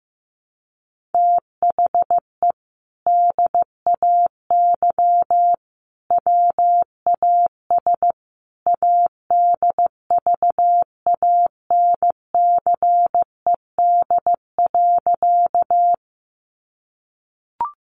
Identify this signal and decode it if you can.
Morse code, heard as a keyed tone, about 15 wpm, 700 Hz. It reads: THE DAY WAS ADVANCED.
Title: Sentences from Top 1200 Words 15wpm